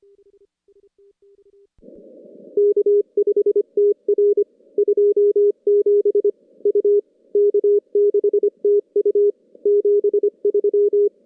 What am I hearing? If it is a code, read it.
Morse code, heard as a keyed tone, about 23 wpm, 410 Hz. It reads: K5TR 27 U K6TU 73
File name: contests / CW-SS-k5tr- -2100